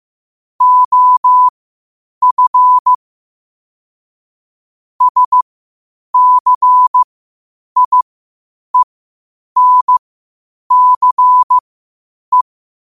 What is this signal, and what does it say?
Morse code, heard as a keyed tone, about 15 wpm, spaced as for 8 wpm, 1000 Hz. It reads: OF SCIENCE
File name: qotdfiles / 8wpm